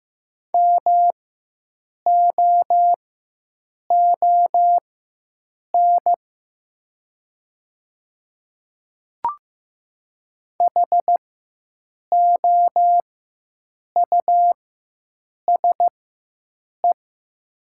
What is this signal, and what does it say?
Morse code, heard as a keyed tone, about 15 wpm, spaced as for 7 wpm, 690 Hz. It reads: MOON HOUSE
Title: 4x Character Spacing - Top 500 Words - 15wpm